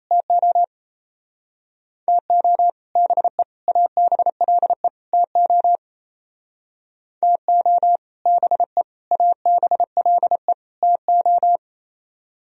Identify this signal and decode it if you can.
Morse code, heard as a keyed tone, about 33 wpm, 700 Hz. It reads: TO TO BE ABLE TO TO BE ABLE TO
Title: Speed-Racing - Sets of 4 Words 28wpm